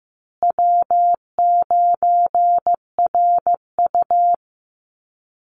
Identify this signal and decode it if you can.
Morse code, heard as a keyed tone, about 15 wpm, 700 Hz. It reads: W9RU
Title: USA Call Signs Repeated 2x 15wpm